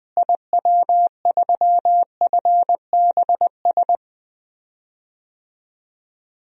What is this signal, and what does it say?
Morse code, heard as a keyed tone, about 20 wpm, 700 Hz. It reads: IW3FBS
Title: DXCC-100 Call Signs - 6 Characters - Repeated 2x - 20wpm